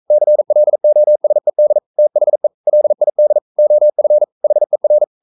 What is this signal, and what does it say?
Morse code, heard as a keyed tone, about 42 wpm, 600 Hz. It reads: CLOSED THE LID OF HER
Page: Romanen Never af Ken Follet på engelsk.